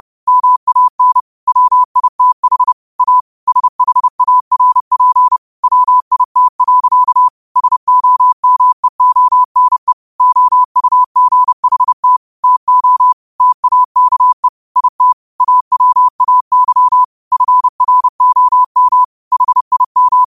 Morse code, heard as a keyed tone, about 30 wpm, 1000 Hz. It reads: MAN WITH A SHARP WIT. SOMEONE OUGHT TO TAKE IT AWAY FROM HIM
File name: 30wpm